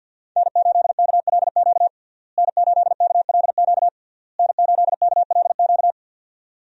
Morse code, heard as a keyed tone, about 49 wpm, 700 Hz. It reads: N8KLQ N8KLQ N8KLQ
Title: USA Call Signs - 5 Characters - Repeated 3x - 50wpm